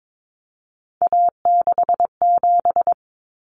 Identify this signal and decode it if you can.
Morse code, heard as a keyed tone, about 22 wpm, 700 Hz. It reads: A67
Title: Common Cut Numbers - 3-Digit 22wpm